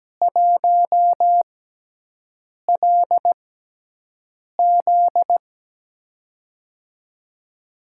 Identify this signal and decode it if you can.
Morse code, heard as a keyed tone, about 17 wpm, spaced as for 6 wpm, 700 Hz. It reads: 1LZ
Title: USA Call Signs - 5 Characters - 6x Character Spacing - 17wpm